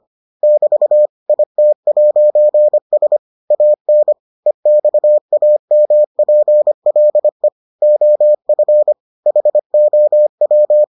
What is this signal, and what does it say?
Morse code, heard as a keyed tone, about 25 wpm, 600 Hz. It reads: <BT> IT'S AN EXAMPLE OF HOW